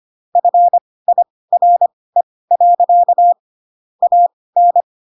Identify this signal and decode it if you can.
Morse code, heard as a keyed tone, about 25 wpm, spaced as for 18 wpm, 700 Hz. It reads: FIRE. AN
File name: Pinocchio / Pinocchio ch6 12 25 ik1hge